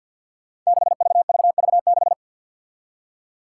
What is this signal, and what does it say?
Morse code, heard as a keyed tone, about 52 wpm, 700 Hz. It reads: 6VVV6
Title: Five Challenging Letter-Number Combinations 50wpm